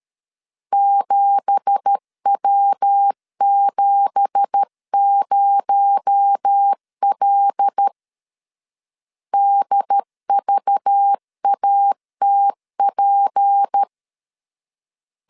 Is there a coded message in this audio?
Yes: Morse code from 0.7 to 13.8 s, heard as a keyed tone, about 13 wpm, 800 Hz. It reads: 7W70L DVATP